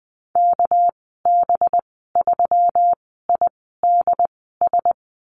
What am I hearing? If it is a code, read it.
Morse code, heard as a keyed tone, about 20 wpm, spaced as for 14 wpm, 700 Hz. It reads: KB3IDS
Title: 207 Lesson 33g - Letter j - Call Signs 20wpm